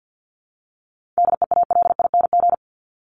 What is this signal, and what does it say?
Morse code, heard as a keyed tone, about 48 wpm, 700 Hz. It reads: DEALING
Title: ING Suffix - Top 100 Words with Base Words 50wpm